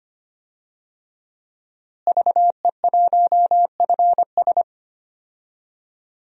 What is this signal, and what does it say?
Morse code, heard as a keyed tone, about 25 wpm, 700 Hz. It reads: VE1FS
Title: DXCC-100 Call Signs - 5 Characters - Repeated 2x - 25wpm